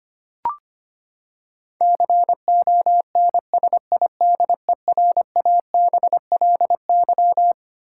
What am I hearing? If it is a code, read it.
Morse code, heard as a keyed tone, about 25 wpm, 700 Hz. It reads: CONSIDERABLY
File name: Twelve Letter Words - 25wpm